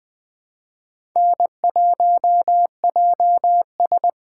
Morse code, heard as a keyed tone, about 20 wpm, 700 Hz. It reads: N1JS
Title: USA Call Signs - 4 Characters - 3x Character Spacing - 20wpm